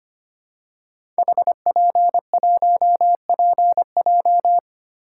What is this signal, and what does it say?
Morse code, heard as a keyed tone, about 25 wpm, 700 Hz. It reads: HP1PJ
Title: Code Groups - 5 Letter:Numbers - 25wpm